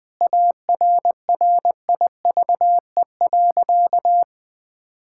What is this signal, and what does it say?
Morse code, heard as a keyed tone, about 21 wpm, 700 Hz. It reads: ARRIVE.
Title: Sentences with 3-Word Phrases - Spoken Beginning - 20wpm